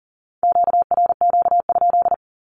Morse code, heard as a keyed tone, about 40 wpm, 700 Hz. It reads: QRQ?
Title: Q-Codes with Spoken Meaning - 40wpm